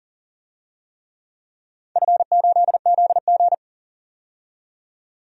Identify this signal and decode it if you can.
Morse code, heard as a keyed tone, about 40 wpm, 700 Hz. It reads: F8ZG
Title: DXCC-100 Call Signs - 4 Characters - 40wpm